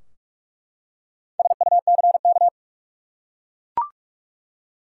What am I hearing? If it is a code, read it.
Morse code, heard as a keyed tone, about 42 wpm, 700 Hz. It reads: SUCK
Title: Four Letter Words - 45wpm